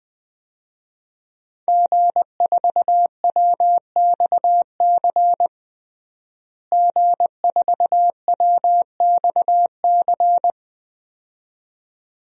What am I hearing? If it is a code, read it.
Morse code, heard as a keyed tone, about 20 wpm, 700 Hz. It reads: G4WXC G4WXC
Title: DXCC-100 Call Signs - 5 Characters - Repeated 2x - 20wpm